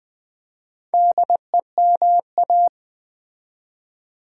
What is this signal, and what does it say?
Morse code, heard as a keyed tone, about 20 wpm, 700 Hz. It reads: DEMA
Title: Build A Word - 2 Letters At A Time - 6-Letter Words - 20wpm